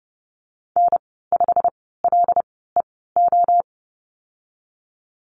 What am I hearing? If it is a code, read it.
Morse code, heard as a keyed tone, about 30 wpm, spaced as for 17 wpm, 700 Hz. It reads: N5LEO
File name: Lesson 15g - Number 5 - Call Signs 30wpm